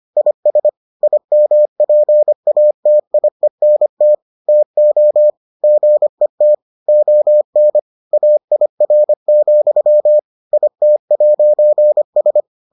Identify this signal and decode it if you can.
Morse code, heard as a keyed tone, about 25 wpm, 600 Hz. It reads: IS IMPATIENT TO GET ON AIR, IT'S